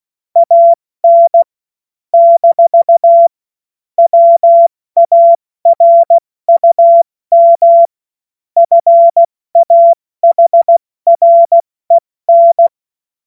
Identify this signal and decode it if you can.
Morse code, heard as a keyed tone, about 16 wpm, 680 Hz. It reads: AN - WARUM FAHREN
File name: Morsekurs / Witze / Witz 13. Januar 2026 70 F80